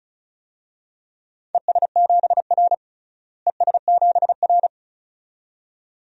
Morse code, heard as a keyed tone, about 36 wpm, 700 Hz. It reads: ES7R ES7R
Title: DXCC-100 Call Signs - 4 Characters - Repeated 2x - 35wpm